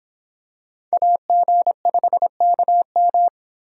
Morse code, heard as a keyed tone, about 26 wpm, 700 Hz. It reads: AG5KM